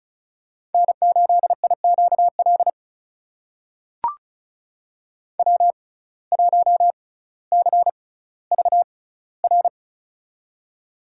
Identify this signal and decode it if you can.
Morse code, heard as a keyed tone, about 36 wpm, spaced as for 18 wpm, 700 Hz. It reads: N8IQL W1CVR
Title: USA Call Signs - 5 Characters - 6x Character Spacing - 35wpm